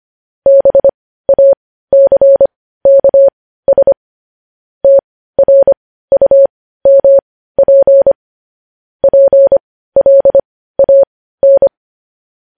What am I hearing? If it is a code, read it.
Morse code, heard as a keyed tone, about 25 wpm, spaced as for 15 wpm, 550 Hz. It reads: BACKS TRUMP PLAN
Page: News Headlines in Morse Code at 15 WPM